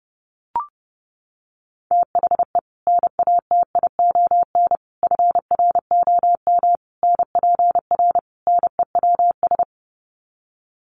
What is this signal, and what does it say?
Morse code, heard as a keyed tone, about 30 wpm, 700 Hz. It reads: THE NATION FROM NPR NEWS
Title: Sets of 5 Words 30wpm